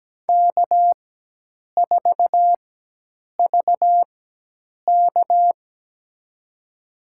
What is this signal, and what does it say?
Morse code, heard as a keyed tone, about 17 wpm, spaced as for 8 wpm, 700 Hz. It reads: K4VK
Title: USA Call Signs - 4 Characters - 4x Character Spacing - 17wpm